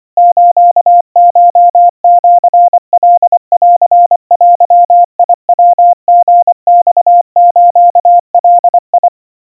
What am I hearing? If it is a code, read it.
Morse code (Japanese Wabun code), heard as a keyed tone, about 25 wpm, 700 Hz. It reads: スコシカンデヤリマスガ
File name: 90 cpm 700 Hz Crispy